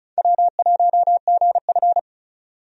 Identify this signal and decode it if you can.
Morse code, heard as a keyed tone, about 34 wpm, 700 Hz. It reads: W1GF
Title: USA Call Signs - 4 Characters - 35wpm